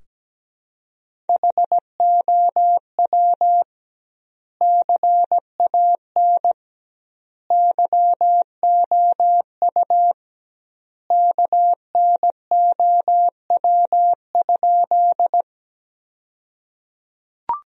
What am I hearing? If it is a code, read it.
Morse code, heard as a keyed tone, about 17 wpm, 700 Hz. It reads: HOW CAN YOU KNOW?
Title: 2x Word-Spacing - Sentences from Top 100 Words - 17wpm